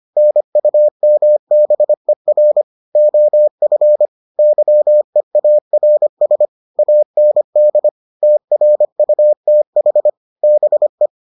Morse code, heard as a keyed tone, about 25 wpm, 600 Hz. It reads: NUMBER OF YEARS AND TRUTH BE